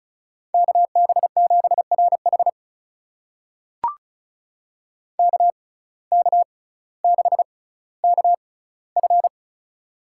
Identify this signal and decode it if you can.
Morse code, heard as a keyed tone, about 34 wpm, spaced as for 18 wpm, 700 Hz. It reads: KB7RH KK6KF
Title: USA Call Signs - 6x Character Spacing - 35wpm